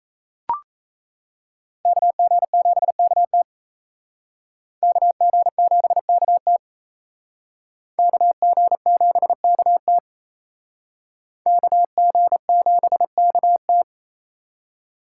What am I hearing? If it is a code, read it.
Morse code, heard as a keyed tone, about 37 wpm, 700 Hz. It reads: KG7KT KG7KT KG7KT KG7KT